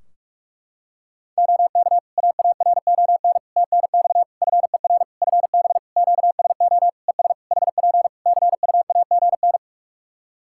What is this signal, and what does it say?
Morse code, heard as a keyed tone, about 44 wpm, 700 Hz. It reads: OK AARON TNX FER FB QSO ES HP CUAGN
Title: K1USN SST QSO Practice 45wpm